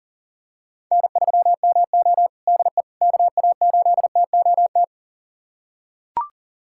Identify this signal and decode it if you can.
Morse code, heard as a keyed tone, about 40 wpm, 700 Hz. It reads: N3MO DE KA8TOT
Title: USA Callsign DE Callsign 40wpm